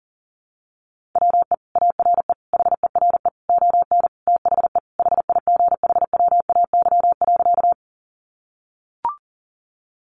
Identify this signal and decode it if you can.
Morse code, heard as a keyed tone, about 40 wpm, 700 Hz. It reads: WE ARE HERE ON THE HIGHWAY.
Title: Sentences from Top 1400 Words 40wpm